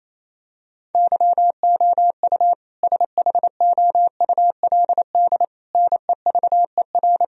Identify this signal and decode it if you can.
Morse code, heard as a keyed tone, about 27 wpm, 700 Hz. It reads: YOU SHOULD NEVER
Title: Sentences from Top 1000 Words - Review Part 3 - 28wpm